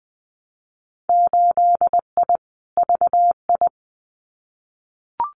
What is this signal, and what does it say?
Morse code, heard as a keyed tone, about 20 wpm, 700 Hz. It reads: 8I VI
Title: Winter Field Day Exchanges Encore 20wpm